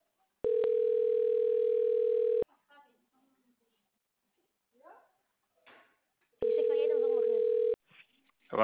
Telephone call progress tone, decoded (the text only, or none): ringback tone